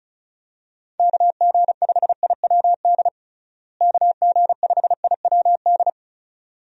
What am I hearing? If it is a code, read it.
Morse code, heard as a keyed tone, about 36 wpm, 700 Hz. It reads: KG5IWD KG5IWD
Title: USA Call Signs - 6 Characters - Repeated 2x - 35wpm